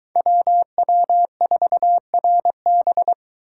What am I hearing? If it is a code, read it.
Morse code, heard as a keyed tone, about 22 wpm, 700 Hz. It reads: WW4RB